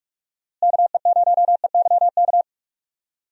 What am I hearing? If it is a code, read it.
Morse code, heard as a keyed tone, about 46 wpm, 700 Hz. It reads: KE0EYK
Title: USA Call Signs - 6 Characters - Repeated 3x - 45wpm